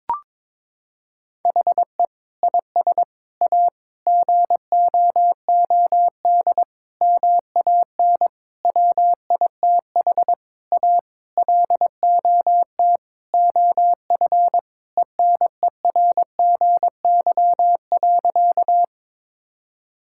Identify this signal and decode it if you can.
Morse code, heard as a keyed tone, about 22 wpm, 700 Hz. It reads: HE IS A GOOD MAN WITH A LOT OF ENERGY.
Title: Sentences from Top 1000 Words - Review Part 6 - 22wpm